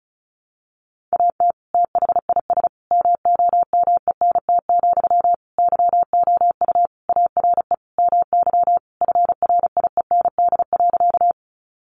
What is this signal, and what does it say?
Morse code, heard as a keyed tone, about 34 wpm, 700 Hz. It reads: AT THIS MOMENT, YOU ARE MY FRIEND.
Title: Sentences from Top 1000 Words 35wpm